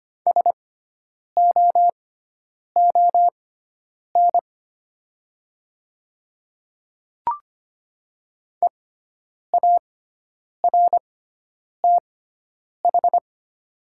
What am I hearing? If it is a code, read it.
Morse code, heard as a keyed tone, about 25 wpm, spaced as for 9 wpm, 700 Hz. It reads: SOON EARTH
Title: 6x Character Spacing - Top 500 Words - 25wpm